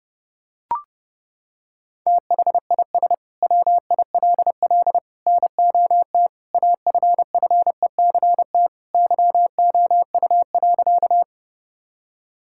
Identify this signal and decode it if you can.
Morse code, heard as a keyed tone, about 30 wpm, 700 Hz. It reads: THIS WILL NOT AFFECT YOU.
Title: Sentences from Top 1000 Words - Review Part 6 - 30wpm